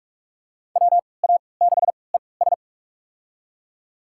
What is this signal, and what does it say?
Morse code, heard as a keyed tone, about 46 wpm, spaced as for 26 wpm, 700 Hz. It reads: WA6ES
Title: USA Call Signs - 5 Characters - 3x Character Spacing - 45wpm